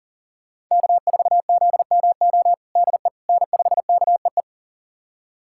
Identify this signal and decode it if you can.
Morse code, heard as a keyed tone, about 40 wpm, 700 Hz. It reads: K4ZMO DE N5KEE
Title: USA Callsign DE Callsign - 5 Characters - 40wpm